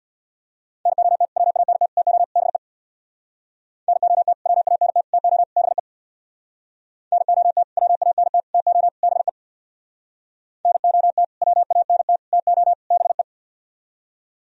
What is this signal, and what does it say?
Morse code, heard as a keyed tone, about 64 wpm, 700 Hz. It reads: NOT WANT TO BE NOT WANT TO BE NOT WANT TO BE NOT WANT TO BE